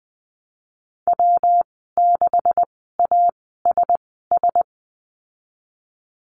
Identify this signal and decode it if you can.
Morse code, heard as a keyed tone, about 20 wpm, spaced as for 15 wpm, 700 Hz. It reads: W6ASS
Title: 193 Lesson 31g - Letter v - Call Signs 20wpm